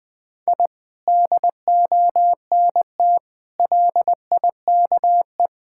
Morse code, heard as a keyed tone, about 20 wpm, 700 Hz. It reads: I DONT LIKE
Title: Sentences with 2-Word Phrases - Spoken Beginning - 20wpm